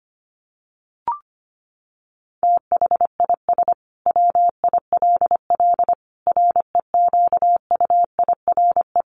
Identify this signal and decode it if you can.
Morse code, heard as a keyed tone, about 25 wpm, 700 Hz. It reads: THIS WILL REQUIRE
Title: Sentences from Top 900 Words 25wpm